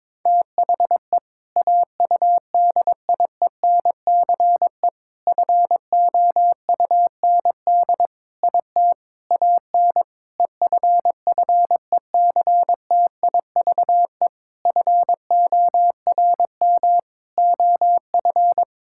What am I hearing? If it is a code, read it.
Morse code, heard as a keyed tone, about 22 wpm, 700 Hz. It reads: THE AUDIENCE FOUND IT AN EFFECTIVE FORM OF
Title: Sentences from Top 1000 Words - Review Part 6 - 22wpm